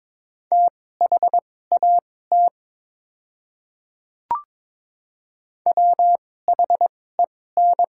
Morse code, heard as a keyed tone, about 22 wpm, spaced as for 16 wpm, 700 Hz. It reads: THAT WHEN
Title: 2x Character Spacing - Top 100 Words - 22wpm